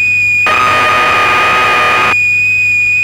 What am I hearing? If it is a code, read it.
Cassette data (Kansas City type).
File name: Acorn User/1984-12 Electron Programs